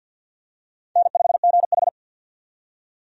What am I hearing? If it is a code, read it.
Morse code, heard as a keyed tone, about 46 wpm, 700 Hz. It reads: N5GH
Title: USA Call Signs - 4 Characters - 50wpm